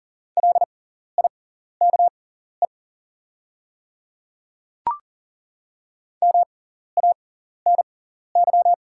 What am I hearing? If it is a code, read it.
Morse code, heard as a keyed tone, about 39 wpm, spaced as for 14 wpm, 690 Hz. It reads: LIKE MANY
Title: 6x Character Spacing - Top 100 Words - 40wpm